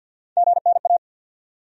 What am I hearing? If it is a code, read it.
Morse code, heard as a keyed tone, about 45 wpm, 700 Hz. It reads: GNA
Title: USA Call Sign Postfixes - 3 Letters 50wpm